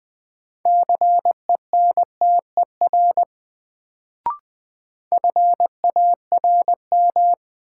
Morse code, heard as a keyed tone, about 20 wpm, 700 Hz. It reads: CENTER FARM
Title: Sentences from Top 1000 Words - Review Part 2 - New Word Drill - 20wpm